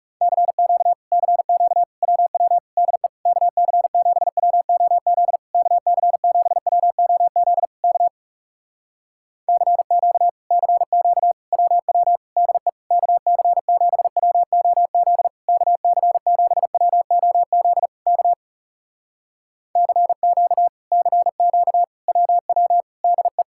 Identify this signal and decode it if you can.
Morse code, heard as a keyed tone, about 40 wpm, 700 Hz. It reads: CQ CQ WW DE KC7WOZ KC7WOZ K CQ CQ WW DE KC7WOZ KC7WOZ K CQ CQ WW DE